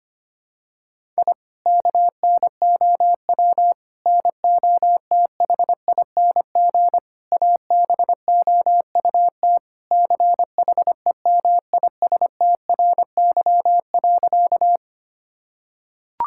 Morse code, heard as a keyed tone, about 25 wpm, 700 Hz. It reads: I KNOW NOTHING ABOUT CHEMISTRY.
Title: Sentences from Top 1000 Words - Review Part 6 - 25wpm